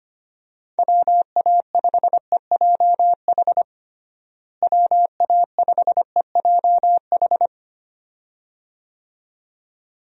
Morse code, heard as a keyed tone, about 25 wpm, 700 Hz. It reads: WA5EJH WA5EJH
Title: USA Call Signs - 6 Characters - Repeated 2x - 25wpm